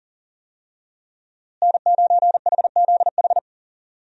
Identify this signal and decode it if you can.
Morse code, heard as a keyed tone, about 40 wpm, 700 Hz. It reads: N9HZH
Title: USA Call Signs - 5 Characters - Repeated 3x - 40wpm